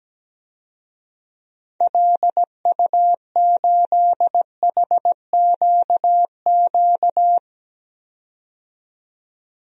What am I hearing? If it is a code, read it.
Morse code, heard as a keyed tone, about 17 wpm, 700 Hz. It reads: LU8HQQ
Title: DXCC-100 Call Signs - 6 Characters - 17wpm